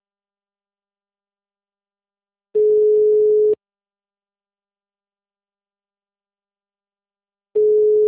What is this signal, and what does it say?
Ringback tone (Europe, 425 Hz).